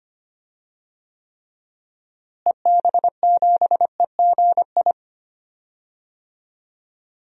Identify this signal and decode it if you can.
Morse code, heard as a keyed tone, about 25 wpm, 700 Hz. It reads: EB7EGI
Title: DXCC-100 Call Signs - 6 Characters - Repeated 3x - 25wpm